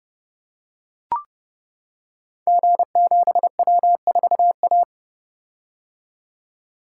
Morse code, heard as a keyed tone, about 28 wpm, 700 Hz. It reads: G7W4A
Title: Code Groups - 5 Letter:Numbers - 30wpm